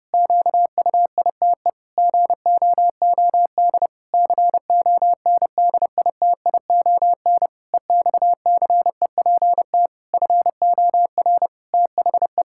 Morse code, heard as a keyed tone, about 30 wpm, 700 Hz. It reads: QUITE GOOD CONDITION EXCEPT FOR THE
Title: Pangram Challenge v2 - 30wpm